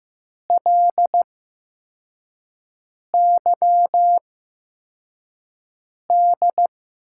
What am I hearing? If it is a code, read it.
Morse code, heard as a keyed tone, about 15 wpm, spaced as for 4 wpm, 700 Hz. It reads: LYD
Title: USA Call Signs - 6 Characters - 8x Character Spacing - 15wpm